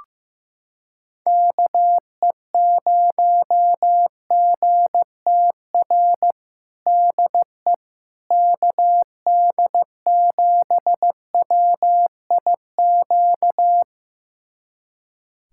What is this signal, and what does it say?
Morse code, heard as a keyed tone, about 16 wpm, 700 Hz. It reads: KE0GTR DE KD7WIQ
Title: USA Callsign DE Callsign - 6 Characters - 15wpm